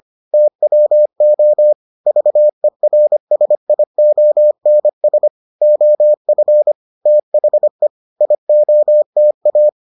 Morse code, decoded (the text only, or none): TWO VERSIONS OF THE IOTA